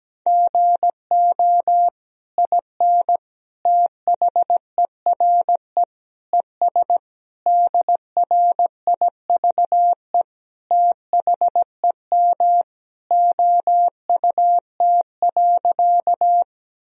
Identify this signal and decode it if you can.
Morse code, heard as a keyed tone, about 17 wpm, 700 Hz. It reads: GO IN THERE ES DRIVE THEM OUT.
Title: Sentences from Top 500 Words and Abbreviations 17wpm